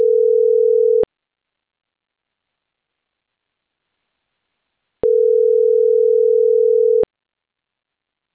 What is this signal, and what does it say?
Ringback tone (North America), 440 + 480 Hz.